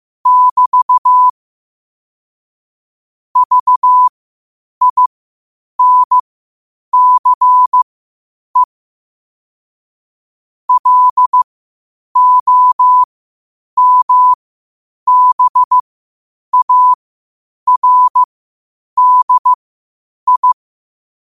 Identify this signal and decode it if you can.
Morse code, heard as a keyed tone, about 15 wpm, spaced as for 8 wpm, 1000 Hz. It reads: <BT> VINCE LOMBARDI
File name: qotdfiles / 8wpm